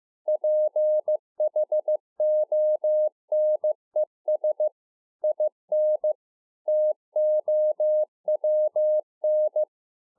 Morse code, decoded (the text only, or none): PHONES IN TOWN